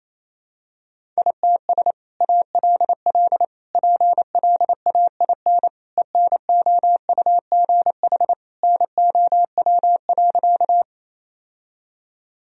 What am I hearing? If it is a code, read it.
Morse code, heard as a keyed tone, about 28 wpm, 700 Hz. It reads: ITS ALL PLAIN ENOUGH NOW.
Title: Sentences from Top 1000 Words - Review Part 2 - 28wpm